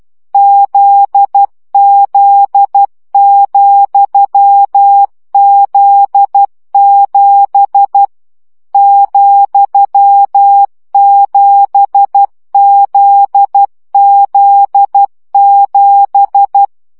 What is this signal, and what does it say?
Morse code, heard as a keyed tone, about 12 wpm, 800 Hz. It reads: ZZ,Z7 ,7ZZ7
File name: CW gyakorlas z 7 vesszo 10 baud